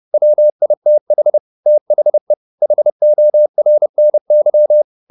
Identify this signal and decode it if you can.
Morse code, heard as a keyed tone, about 30 wpm, 600 Hz. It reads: WITH THE HORNY
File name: the old man and the sea / the old man and the sea 1 30wpm